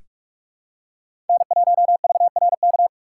Morse code, heard as a keyed tone, about 46 wpm, 700 Hz. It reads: N1VRK
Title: USA Call Signs - 5 Characters - 4x Character Spacing - 45wpm